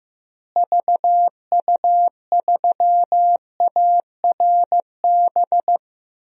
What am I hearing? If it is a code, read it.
Morse code, heard as a keyed tone, about 15 wpm, 700 Hz. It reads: VU3ARB
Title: DXCC-100 Call Signs - 6 Characters - Repeated 2x - 15wpm